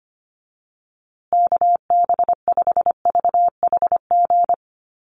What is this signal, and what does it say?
Morse code, heard as a keyed tone, about 25 wpm, 700 Hz. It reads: KB5VHG
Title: USA Call Signs Repeated 3x 25wpm